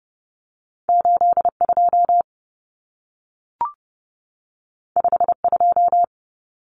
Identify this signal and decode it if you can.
Morse code, heard as a keyed tone, about 30 wpm, 700 Hz. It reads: 82 52
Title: Common Cut Numbers - 2-Digit 30wpm